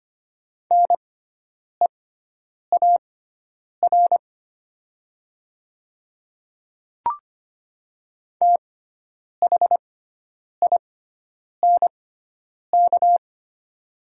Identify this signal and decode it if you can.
Morse code, heard as a keyed tone, about 25 wpm, spaced as for 9 wpm, 700 Hz. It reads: NEAR THINK
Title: 6x Character Spacing - Top 200 Words - 25wpm